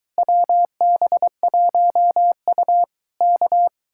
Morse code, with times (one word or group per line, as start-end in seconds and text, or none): 0.2-2.8 WB1U
3.2-3.7 K